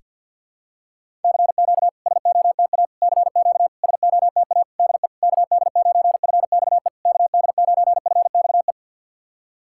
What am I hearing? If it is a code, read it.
Morse code, heard as a keyed tone, about 50 wpm, 690 Hz. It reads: CQ SOTA CQ SOTA DE KD9FXE KD9FXE
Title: SOTA QSO Practice 50wpm